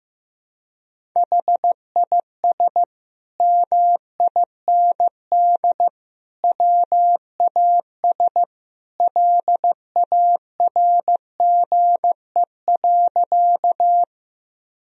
Morse code, heard as a keyed tone, about 15 wpm, 700 Hz. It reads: HIS MIND WAS LARGE.
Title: Sentences from Top 1000 Words - Review Part 1 - 15wpm